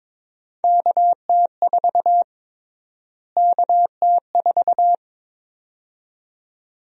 Morse code, heard as a keyed tone, about 21 wpm, 700 Hz. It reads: KT4 KT4
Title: USA Call Sign Prefixes - Repeated 2x - 22wpm